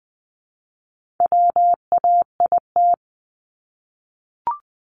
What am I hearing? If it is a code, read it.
Morse code, heard as a keyed tone, about 20 wpm, 700 Hz. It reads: WAIT
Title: ING Suffix - Top 100 Words with Base Words 20wpm